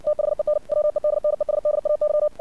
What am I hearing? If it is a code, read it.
Morse code, heard as a keyed tone, about 60 wpm, 600 Hz. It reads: THEN WEDNESDAY
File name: archived / CRSnet